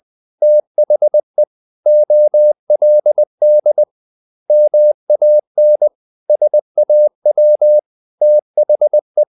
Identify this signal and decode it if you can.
Morse code, heard as a keyed tone, about 20 wpm, 600 Hz. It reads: THE OLD MAN SAW THE
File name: the old man and the sea / the old man and the sea part 1 20wpm